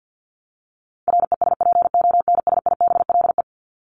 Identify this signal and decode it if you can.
Morse code, heard as a keyed tone, about 50 wpm, 700 Hz. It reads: RESPONSIBLE
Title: Top 751-1000 Words 50wpm